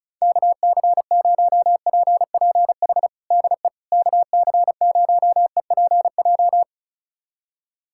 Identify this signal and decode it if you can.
Morse code, heard as a keyed tone, about 34 wpm, 700 Hz. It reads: KC0PPH DE KC0EPJ
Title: USA Callsign DE Callsign 35wpm